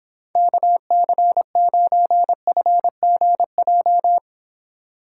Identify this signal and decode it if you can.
Morse code, heard as a keyed tone, about 26 wpm, 700 Hz. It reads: KC9FGJ